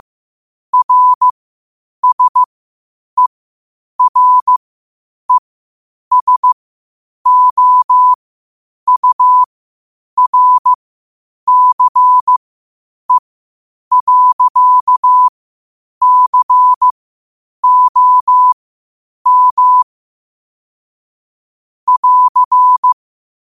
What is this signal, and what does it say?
Morse code, heard as a keyed tone, about 15 wpm, spaced as for 8 wpm, 1000 Hz. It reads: RSERESOURCE.COM <AR>
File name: qotdfiles / 8wpm